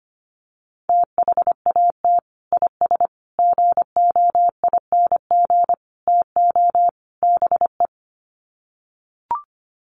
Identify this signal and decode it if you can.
Morse code, heard as a keyed tone, about 25 wpm, 700 Hz. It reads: THAT IS GOING TO BE
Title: Sets of 5 Words Encore2 25wpm